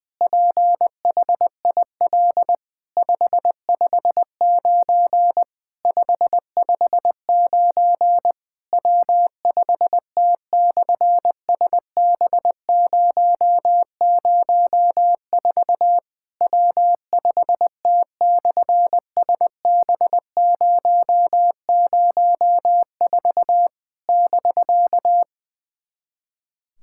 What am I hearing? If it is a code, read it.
Morse code, heard as a keyed tone, about 20 wpm, 700 Hz. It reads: PHIL 559 559 W5T/SB004 W5T/SB004 <BK>
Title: SOTA QSO Practice 20wpm